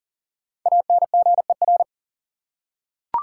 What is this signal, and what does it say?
Morse code, heard as a keyed tone, about 40 wpm, 700 Hz. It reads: ANGER
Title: Five Letter Words - 40wpm